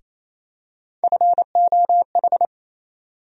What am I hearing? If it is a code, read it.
Morse code, heard as a keyed tone, about 28 wpm, 700 Hz. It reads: FOH
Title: USA Call Sign Postfixes - 3 Letters 28wpm